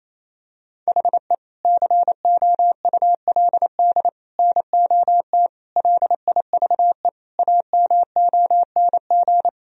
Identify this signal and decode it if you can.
Morse code, heard as a keyed tone, about 28 wpm, 700 Hz. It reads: HE COULD NOT LIVE AMONG
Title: Sentences from Top 1000 Words - Review Part 3 - 28wpm